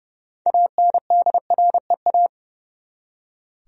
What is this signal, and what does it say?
Morse code, heard as a keyed tone, about 30 wpm, 700 Hz. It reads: ANDREA
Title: USA Common Names - 30wpm